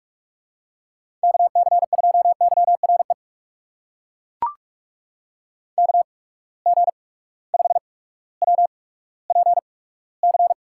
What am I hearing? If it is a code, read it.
Morse code, heard as a keyed tone, about 42 wpm, spaced as for 19 wpm, 700 Hz. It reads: KC2YRE KG5WPC
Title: USA Call Signs - 6 Characters - 8x Character Spacing - 45wpm